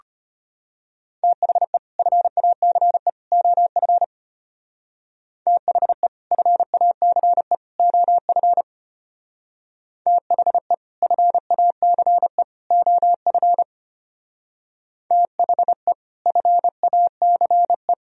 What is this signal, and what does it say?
Morse code, heard as a keyed tone, about 32 wpm, 700 Hz. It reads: THE FACE OF THE FACE OF THE FACE OF THE FACE